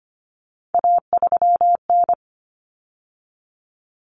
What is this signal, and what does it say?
Morse code, heard as a keyed tone, about 25 wpm, 700 Hz. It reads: A3N
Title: Common Cut Numbers - 3-Digit 25wpm